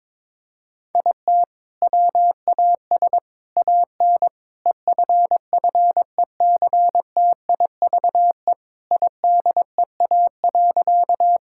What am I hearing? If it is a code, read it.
Morse code, heard as a keyed tone, about 22 wpm, 700 Hz. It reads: IT WAS AN EFFECTIVE IDEA.
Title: Sentences from Top 1000 Words - Review Part 6 - 22wpm